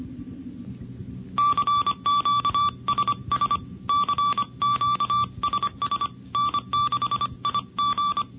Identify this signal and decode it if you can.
Morse code, heard as a keyed tone, about 25 wpm, 1160 Hz. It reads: CQSS CQSS N6IG
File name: NCCC sweepstakes